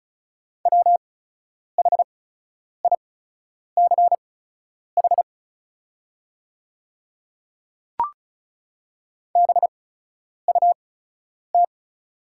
Morse code, heard as a keyed tone, about 36 wpm, spaced as for 10 wpm, 700 Hz. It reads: WHICH BUT
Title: 8x Character Spacing - Top 100 Words - 35wpm